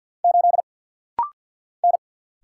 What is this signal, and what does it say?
Morse code, heard as a keyed tone, about 50 wpm, 700 Hz. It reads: 8 N
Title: Single Letter-Number - Rapid-Fire 50wpm